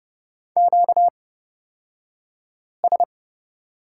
Morse code, heard as a keyed tone, about 30 wpm, spaced as for 5 wpm, 700 Hz. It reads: QS
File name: Lesson 38d - Letter q - Two Characters 30wpm